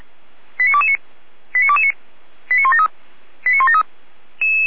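Five-tone selective call (ZVEI), heard as quick runs of five tones, tones of about 70 ms. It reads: 89229 89229 89173 89173
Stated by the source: ZVEI five-tone selective call received on SDR, from Feuerwehr Orlamünde.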